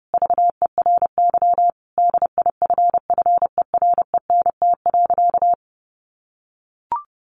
Morse code, heard as a keyed tone, about 30 wpm, 700 Hz. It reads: VERY DIFFERENT.
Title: Sentences from Top 1300 Words 30wpm